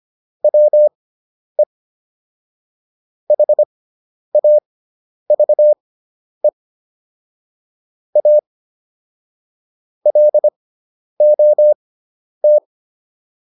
Morse code, heard as a keyed tone, about 25 wpm, spaced as for 10 wpm, 600 Hz. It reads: WE HAVE A LOT